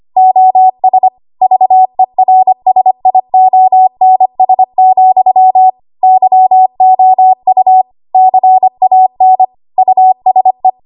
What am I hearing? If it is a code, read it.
Morse code, heard as a keyed tone, about 25 wpm, 750 Hz. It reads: OS VERSIONS, YOU CAN USE
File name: Archive / 25 WPM / 25WPM